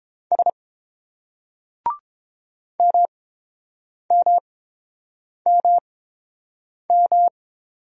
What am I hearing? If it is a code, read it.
Morse code, heard as a keyed tone, about 30 wpm, spaced as for 7 wpm, 690 Hz. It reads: S MMMM